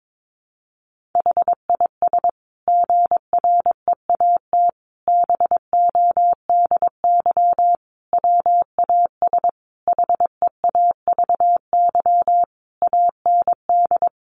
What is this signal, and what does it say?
Morse code, heard as a keyed tone, about 22 wpm, 700 Hz. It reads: HIS GREAT BODY WAS HEAVY AND
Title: Sentences from Top 1400 Words 22wpm